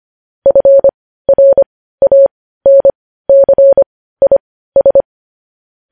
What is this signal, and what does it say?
Morse code, heard as a keyed tone, about 25 wpm, spaced as for 15 wpm, 550 Hz. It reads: FRANCIS
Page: news headlines in morse code at 15 wpm